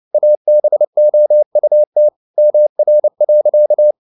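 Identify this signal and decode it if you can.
Morse code, heard as a keyed tone, about 29 wpm, 600 Hz. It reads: ABOUT MR.
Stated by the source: Morse code practice at 28 wpm, text of Ken Follett's novel Never in English, from CW med Gnister.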